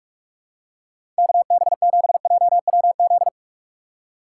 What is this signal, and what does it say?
Morse code, heard as a keyed tone, about 44 wpm, 700 Hz. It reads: KB7JWZ
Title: USA Call Signs - 6 Characters - 6x Character Spacing - 45wpm